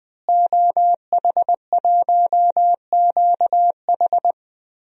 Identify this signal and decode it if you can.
Morse code, heard as a keyed tone, about 20 wpm, 700 Hz. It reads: OH1QH
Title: DXCC-100 Call Signs - 5 Characters - Repeated 3x - 20wpm